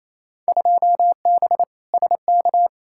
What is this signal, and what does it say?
Morse code, heard as a keyed tone, about 28 wpm, 700 Hz. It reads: 2B SK
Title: Field Day Exchanges Encore - 28wpm